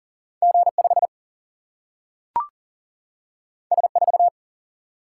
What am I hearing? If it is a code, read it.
Morse code, heard as a keyed tone, about 39 wpm, 700 Hz. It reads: G 5 S 4
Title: Single-Letter Single-Number 40wpm